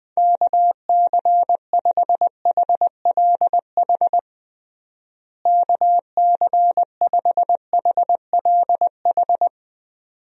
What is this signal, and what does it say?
Morse code, heard as a keyed tone, about 20 wpm, 700 Hz. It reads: KC5HLH KC5HLH
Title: USA Call Signs - 6 Characters - Repeated 2x - 20wpm